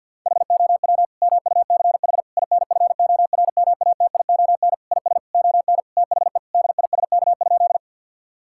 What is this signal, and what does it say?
Morse code, heard as a keyed tone, about 50 wpm, 700 Hz. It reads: HOW MUCH INFORMATION IS ON THE DISK?
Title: Sentences from Top 1000 Words - Review Part 6 - 50wpm